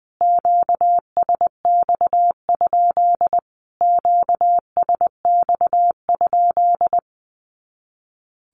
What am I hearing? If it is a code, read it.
Morse code, heard as a keyed tone, about 20 wpm, 700 Hz. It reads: QSX? QSX?
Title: Q-Codes Repeated 3x 20wpm